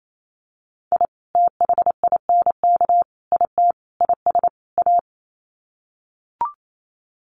Morse code, heard as a keyed tone, about 28 wpm, 700 Hz. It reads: I THINK IT IS A
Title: Sets of 5 Words Encore 28wpm